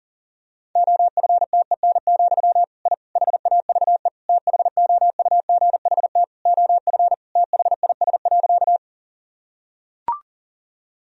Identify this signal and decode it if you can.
Morse code, heard as a keyed tone, about 40 wpm, 700 Hz. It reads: OFTEN, I HAVE THOUGHT OF THIS.
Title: Sentences from Top 1000 Words - Review Part 1 - 40wpm